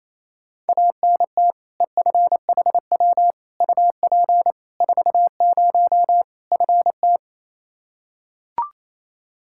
Morse code, heard as a keyed tone, about 28 wpm, 700 Hz. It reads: ANT EFHW UP 40 FT
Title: K1USN SST QSO Practice 28wpm